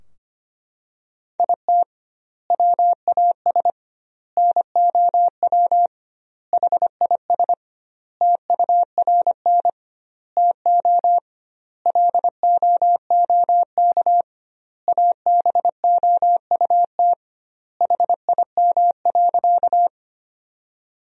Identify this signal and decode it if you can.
Morse code, heard as a keyed tone, about 25 wpm, 700 Hz. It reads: IT WAS NOW HIS TURN TO LOOK ABOUT HIM.
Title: 2x Word-Spacing - Sentences from Top 200 Words - 25wpm